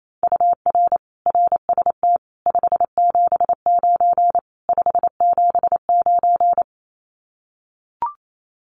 Morse code, heard as a keyed tone, about 28 wpm, 700 Hz. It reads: UR RST 579 579
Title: Common QSO Phrases Encore 28wpm